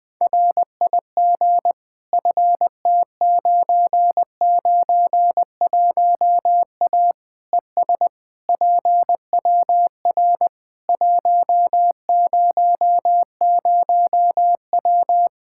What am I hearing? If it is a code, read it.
Morse code, heard as a keyed tone, about 20 wpm, 700 Hz. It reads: RIG FT991A ES PWR 100W
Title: K1USN SST QSO Practice 20wpm